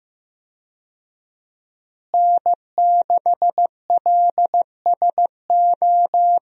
Morse code, heard as a keyed tone, about 15 wpm, 700 Hz. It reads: N6LSO
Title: USA Call Signs - 5 Characters - Repeated 2x - 15wpm